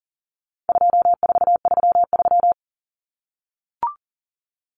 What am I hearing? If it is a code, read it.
Morse code, heard as a keyed tone, about 40 wpm, 700 Hz. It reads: 2433
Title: Common Cut Numbers - 4-Digit 40wpm